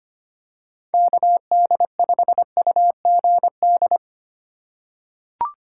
Morse code, heard as a keyed tone, about 25 wpm, 700 Hz. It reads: KD5UGD
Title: USA Call Signs - 4x Character Spacing - 25wpm